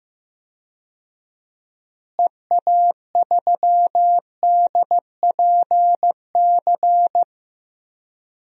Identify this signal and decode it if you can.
Morse code, heard as a keyed tone, about 15 wpm, 700 Hz. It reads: EA3DPC
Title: DXCC-100 Call Signs - 6 Characters - Repeated 3x - 15wpm